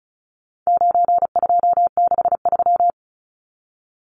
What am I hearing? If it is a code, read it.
Morse code, heard as a keyed tone, about 34 wpm, 700 Hz. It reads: 9263
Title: Numbers - 4-Digit 35wpm